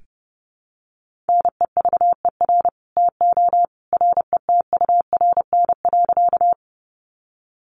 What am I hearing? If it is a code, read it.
Morse code, heard as a keyed tone, about 30 wpm, 700 Hz. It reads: NEVER TO RETURN.
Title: Sentences from Top 800 Words 30wpm